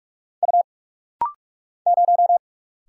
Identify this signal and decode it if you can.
Morse code, heard as a keyed tone, about 45 wpm, 700 Hz. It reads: U 0
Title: Single Letter-Number - Rapid-Fire 45wpm